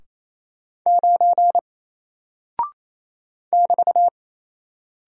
Morse code, heard as a keyed tone, about 27 wpm, 700 Hz. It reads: 9 <BT>
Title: Single Character (Letter:Number:Punctuation:Prosign) 28wpm